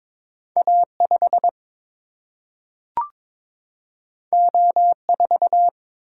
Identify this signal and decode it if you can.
Morse code, heard as a keyed tone, about 22 wpm, 700 Hz. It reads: A5 O4
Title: Single-Letter Single-Number 22wpm